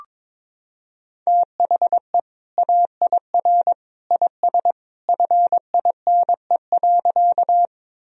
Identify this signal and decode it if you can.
Morse code, heard as a keyed tone, about 22 wpm, 700 Hz. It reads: THE AIR IS FINE.
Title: Sentences from Top 1000 Words - Review Part 1 - 22wpm